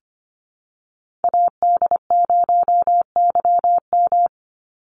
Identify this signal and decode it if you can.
Morse code, heard as a keyed tone, about 25 wpm, 700 Hz. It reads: AD0YM
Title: NA SOTA Chaser Callsigns 25wpm